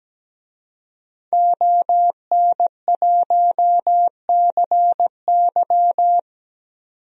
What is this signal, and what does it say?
Morse code, heard as a keyed tone, about 17 wpm, 700 Hz. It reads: ON1CY
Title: DXCC-100 Call Signs - 5 Characters - 17wpm